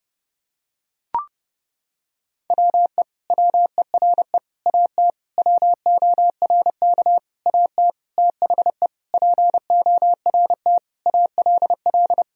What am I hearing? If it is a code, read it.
Morse code, heard as a keyed tone, about 30 wpm, 700 Hz. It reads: WE WERE AT WORK AT THE PORT ALL
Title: Sentences from Top 1000 Words - Review Part 1 - 30wpm